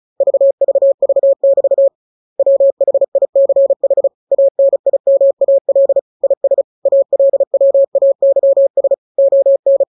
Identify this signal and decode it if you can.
Morse code, heard as a keyed tone, about 34 wpm, 550 Hz. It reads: VVV<BT> WHICH ANIMAL IS ALWAYS ON
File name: jokes / joke 28 35 0000